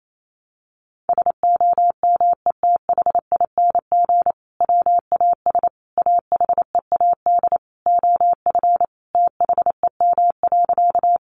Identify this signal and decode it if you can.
Morse code, heard as a keyed tone, about 28 wpm, 700 Hz. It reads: SOMETHING WAS AHEAD OF THEM.
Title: Sentences from Top 1100 Words 28wpm